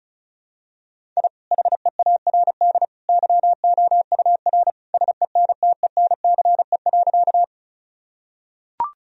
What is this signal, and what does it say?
Morse code, heard as a keyed tone, about 34 wpm, 700 Hz. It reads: I HEARD YOUR SENTENCE.
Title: Sentences from Top 1000 Words - Review Part 1 - 35wpm